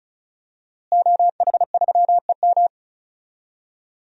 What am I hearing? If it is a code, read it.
Morse code, heard as a keyed tone, about 36 wpm, 700 Hz. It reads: OH3EM
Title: DXCC-100 Call Signs - 5 Characters - Repeated 3x - 35wpm